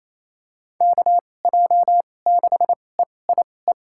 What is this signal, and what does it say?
Morse code, heard as a keyed tone, about 28 wpm, spaced as for 20 wpm, 700 Hz. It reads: KJ6EIE
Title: USA Call Signs - 2x Character Spacing - 28wpm